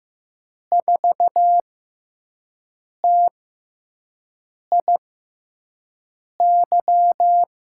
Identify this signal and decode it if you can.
Morse code, heard as a keyed tone, about 15 wpm, spaced as for 5 wpm, 700 Hz. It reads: 4TIY
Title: USA Call Signs - 6 Characters - 6x Character Spacing - 15wpm